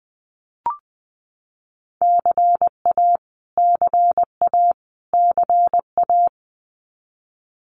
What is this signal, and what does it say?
Morse code, heard as a keyed tone, about 20 wpm, 700 Hz. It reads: CA CA CA
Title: US States Repeated 3x 20wpm